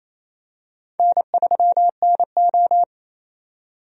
Morse code, heard as a keyed tone, about 28 wpm, 700 Hz. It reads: N3NO
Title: USA Call Signs - 4 Characters - 28wpm